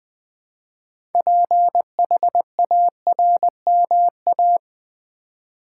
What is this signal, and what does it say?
Morse code, heard as a keyed tone, about 20 wpm, 700 Hz. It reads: PHARMA
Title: Build A Word - 3 Letters At A Time - 8-Letter Words - 20wpm